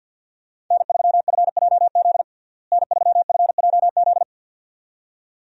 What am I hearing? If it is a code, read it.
Morse code, heard as a keyed tone, about 50 wpm, 700 Hz. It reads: N3FJZ N3FJZ
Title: USA Call Signs - 5 Characters - Repeated 2x - 50wpm